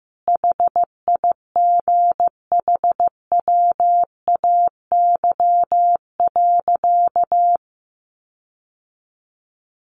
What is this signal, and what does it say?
Morse code, heard as a keyed tone, about 15 wpm, 700 Hz. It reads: HIGHWAY.
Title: Sentences from Top 1400 Words 15wpm